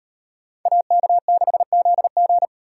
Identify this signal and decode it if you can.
Morse code, heard as a keyed tone, about 36 wpm, 700 Hz. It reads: AK6ZG